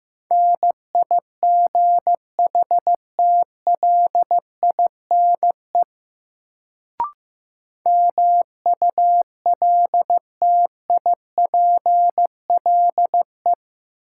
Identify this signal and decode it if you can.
Morse code, heard as a keyed tone, about 15 wpm, 700 Hz. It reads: NIGHTLINE MULTIPLE
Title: Sets of 5 Words - New Word Drill - 15wpm